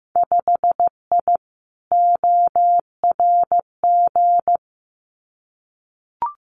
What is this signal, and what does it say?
Morse code, heard as a keyed tone, about 15 wpm, 700 Hz. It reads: 5I ORG
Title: Winter Field Day Exchanges Encore 15wpm